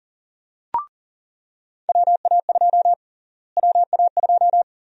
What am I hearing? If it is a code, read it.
Morse code, heard as a keyed tone, about 40 wpm, 700 Hz. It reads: WA2 WA2
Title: USA Call Sign Prefixes - Repeated 2x - 40wpm